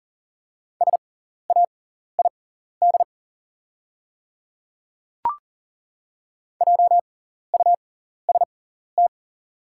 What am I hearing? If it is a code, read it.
Morse code, heard as a keyed tone, about 40 wpm, spaced as for 14 wpm, 700 Hz. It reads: SAID JUST
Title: 6x Character Spacing - Top 200 Words - 40wpm